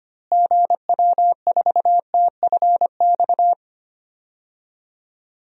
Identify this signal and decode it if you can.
Morse code, heard as a keyed tone, about 25 wpm, 700 Hz. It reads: GW4TFX
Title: DXCC-100 Call Signs - 6 Characters - Repeated 2x - 25wpm